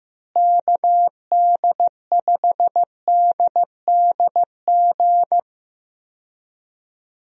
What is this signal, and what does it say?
Morse code, heard as a keyed tone, about 15 wpm, 700 Hz. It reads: KD5DDG
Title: USA Call Signs - 6 Characters - Repeated 2x - 15wpm